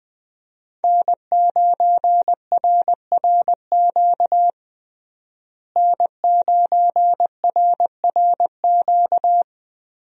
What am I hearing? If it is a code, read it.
Morse code, heard as a keyed tone, about 21 wpm, 700 Hz. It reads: N9RRQ N9RRQ
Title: USA Call Signs - 5 Characters - Repeated 2x - 20wpm